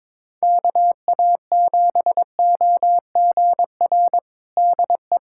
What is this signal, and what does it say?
Morse code, heard as a keyed tone, about 22 wpm, 700 Hz. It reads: KA7OGR DE
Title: USA Callsign DE Callsign - 6 Characters - 22wpm